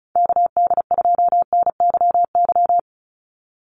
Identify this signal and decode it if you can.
Morse code, heard as a keyed tone, about 34 wpm, 700 Hz. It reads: KD2NYY
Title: USA Call Signs 35wpm